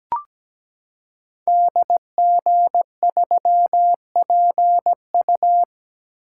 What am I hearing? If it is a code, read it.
Morse code, heard as a keyed tone, about 17 wpm, 700 Hz. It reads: DG3PU
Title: Code Groups - 5 Letter:Numbers - 17wpm